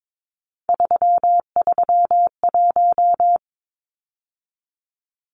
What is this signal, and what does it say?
Morse code, heard as a keyed tone, about 22 wpm, 700 Hz. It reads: 331
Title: Numbers - 3-Digit 22wpm